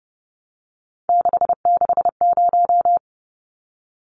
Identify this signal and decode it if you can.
Morse code, heard as a keyed tone, about 30 wpm, 700 Hz. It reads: 660
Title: Numbers - 3-Digit 30wpm